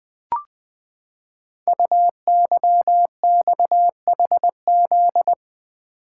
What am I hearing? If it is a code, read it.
Morse code, heard as a keyed tone, about 21 wpm, 700 Hz. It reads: UYXHZ
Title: Code Groups - 5 Letters - 20wpm